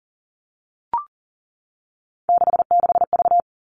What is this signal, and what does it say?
Morse code, heard as a keyed tone, about 40 wpm, 700 Hz. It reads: 66V
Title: All Cut Numbers - 3-Digit 40wpm